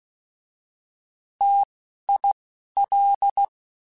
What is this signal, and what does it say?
Morse code, heard as a keyed tone, about 16 wpm, 800 Hz. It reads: T I L